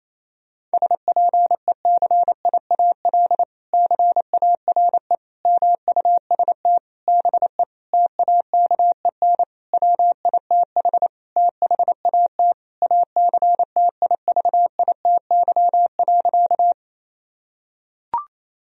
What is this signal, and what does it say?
Morse code, heard as a keyed tone, about 28 wpm, 700 Hz. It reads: SPECIAL CARE MUST BE TAKEN WITH THAT ACTIVITY.
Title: Sentences from Top 1000 Words - Review Part 4 - 28wpm